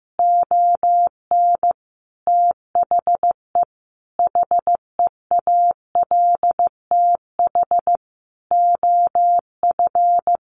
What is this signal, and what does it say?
Morse code, heard as a keyed tone, about 15 wpm, 700 Hz. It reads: ON THE HEALTH OF
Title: Sentences from Top 1500 Words 15wpm